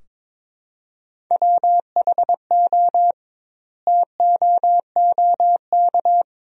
Morse code, heard as a keyed tone, about 22 wpm, 700 Hz. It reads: WHO TOOK
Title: 2x Word-Spacing - Sentences from Top 300 Words - 22wpm